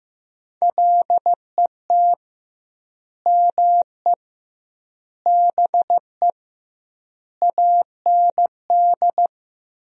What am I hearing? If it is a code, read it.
Morse code, heard as a keyed tone, about 15 wpm, 700 Hz. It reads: LET ME BE AND
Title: 2x Word-Spacing - Sentences from Top 300 Words - 15wpm